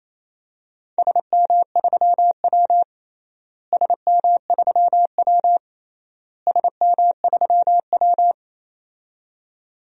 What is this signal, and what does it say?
Morse code, heard as a keyed tone, about 27 wpm, 700 Hz. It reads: SM3W SM3W SM3W
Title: DXCC-100 Call Signs - 4 Characters - Repeated 3x - 28wpm